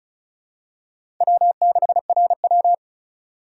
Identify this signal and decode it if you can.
Morse code, heard as a keyed tone, about 36 wpm, 700 Hz. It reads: W6RW
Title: USA Call Signs - 4 Characters - 2x Character Spacing - 35wpm